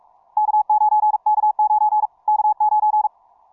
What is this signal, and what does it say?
Morse code, heard as a keyed tone, about 46 wpm, 850 Hz. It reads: K9 K9 K9
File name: simple-file-list / tx5k 15mcw2